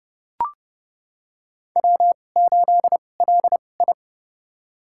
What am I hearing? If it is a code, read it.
Morse code, heard as a keyed tone, about 30 wpm, spaced as for 22 wpm, 700 Hz. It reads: W8LI
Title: USA Call Signs - 4 Characters - 2x Character Spacing - 30wpm